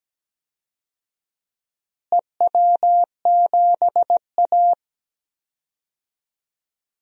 Morse code, heard as a keyed tone, about 17 wpm, 700 Hz. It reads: EW7A
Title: DXCC-100 Call Signs - 4 Characters - Repeated 2x - 17wpm